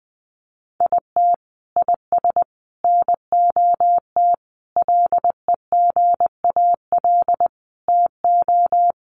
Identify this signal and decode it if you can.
Morse code, heard as a keyed tone, about 20 wpm, 700 Hz. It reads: IT IS NOT LEGAL TO
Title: Sentences from Top 1200 Words 20wpm